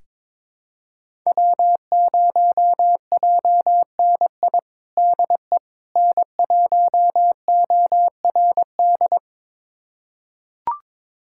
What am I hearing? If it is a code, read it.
Morse code, heard as a keyed tone, about 22 wpm, 700 Hz. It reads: W0JNI DE N1ORD
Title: USA Callsign DE Callsign - 5 Characters - 22wpm